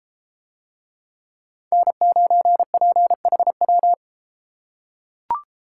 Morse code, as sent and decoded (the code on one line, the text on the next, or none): -. ----. .--. .... .--
N9PHW